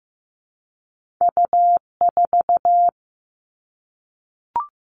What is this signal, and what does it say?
Morse code, heard as a keyed tone, about 15 wpm, 700 Hz. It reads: U4
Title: Two Challenging Letter-Number Combinations 15wpm